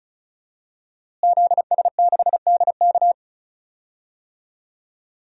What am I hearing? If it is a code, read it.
Morse code, heard as a keyed tone, about 34 wpm, 700 Hz. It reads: ZS6DK
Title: DXCC-100 Call Signs - 5 Characters - Repeated 2x - 35wpm